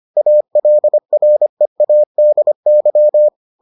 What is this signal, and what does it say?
Morse code, heard as a keyed tone, about 25 wpm, 600 Hz. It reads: ALREADY